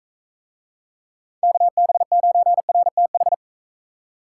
Morse code, heard as a keyed tone, about 42 wpm, 700 Hz. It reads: KB9RTH